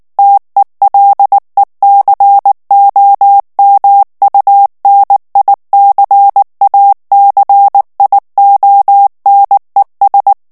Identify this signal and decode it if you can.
Morse code, heard as a keyed tone, about 19 wpm, 800 Hz. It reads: TELECOMUNICACIONES
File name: morse mp3 / ESPAGNOL TEXTE EN CLAIR 2 VITESSE 18 MOTS MINUTES